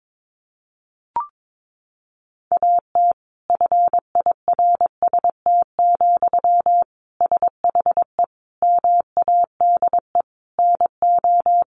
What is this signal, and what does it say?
Morse code, heard as a keyed tone, about 22 wpm, 700 Hz. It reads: AT FIRST, SHE MADE NO
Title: Sentences from Top 1400 Words 22wpm